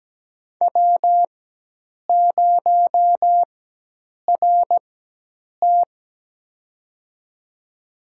Morse code, heard as a keyed tone, about 17 wpm, spaced as for 8 wpm, 700 Hz. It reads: W0RT
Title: USA Call Signs - 4x Character Spacing - 17wpm